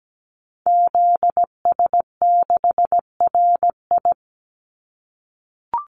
Morse code, heard as a keyed tone, about 17 wpm, 700 Hz. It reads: ZS6RI
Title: Call Signs - 17wpm